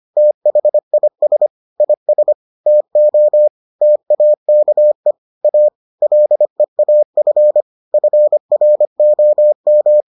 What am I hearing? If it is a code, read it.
Morse code, heard as a keyed tone, about 25 wpm, 600 Hz. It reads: THIS IS TO TAKE A LEAF FROM